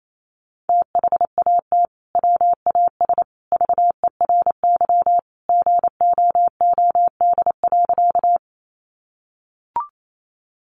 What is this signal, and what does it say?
Morse code, heard as a keyed tone, about 28 wpm, 700 Hz. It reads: THAT WAS VERY GOOD.
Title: Sentences from Top 200 Words 28wpm